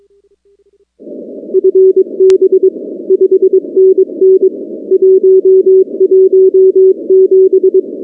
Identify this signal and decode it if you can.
Morse code, heard as a keyed tone, about 22 wpm, 380 Hz. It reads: FB 5NN 117
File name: contests / 2019-cw-wpx-kt5j-k5pi- -2330 0-19